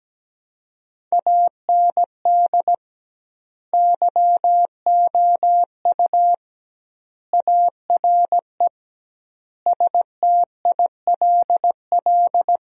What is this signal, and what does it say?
Morse code, heard as a keyed tone, about 17 wpm, 700 Hz. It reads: AND YOU ARE STILL
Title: 2x Word-Spacing - Sentences from Top 400 Words - 17wpm